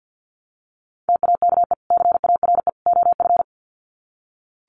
Morse code, heard as a keyed tone, about 48 wpm, 700 Hz. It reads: TAKE CARE OF
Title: Sets of 3 Words 50wpm